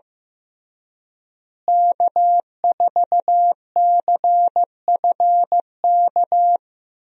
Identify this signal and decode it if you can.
Morse code, heard as a keyed tone, about 16 wpm, 700 Hz. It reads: K4CFK
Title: USA Call Signs - 5 Characters - Repeated 3x - 15wpm